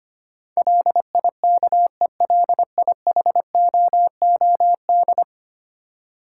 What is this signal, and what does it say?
Morse code, heard as a keyed tone, about 25 wpm, 700 Hz. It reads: LIKELIHOOD
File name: Ten Letter Words - 25wpm